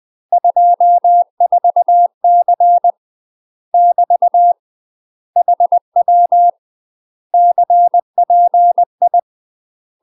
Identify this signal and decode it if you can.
Morse code, heard as a keyed tone, about 20 wpm, 700 Hz. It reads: 24C <BT> HW CPI
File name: UK MASTER QSOs TRAINING / 20WPM / UK MASTER QSOs 0024